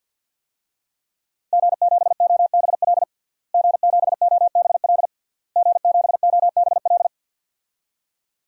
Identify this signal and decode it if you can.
Morse code, heard as a keyed tone, about 47 wpm, 700 Hz. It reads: G7OBL G7OBL G7OBL
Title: DXCC-100 Call Signs - 5 Characters - Repeated 3x - 50wpm